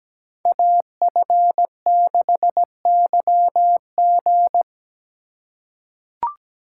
Morse code, heard as a keyed tone, about 17 wpm, 700 Hz. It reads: AF6YG
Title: USA Call Signs - 5 Characters - 4x Character Spacing - 17wpm